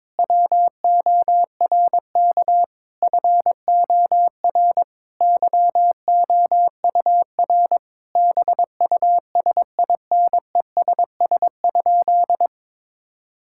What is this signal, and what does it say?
Morse code, heard as a keyed tone, about 22 wpm, 700 Hz. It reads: WORK FOR YOUR BUSINESS?
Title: Sentences from Top 1000 Words - Review Part 6 - 22wpm